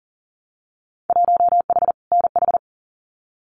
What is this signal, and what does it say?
Morse code, heard as a keyed tone, about 39 wpm, 700 Hz. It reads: 1H NH
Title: Winter Field Day Exchanges 40wpm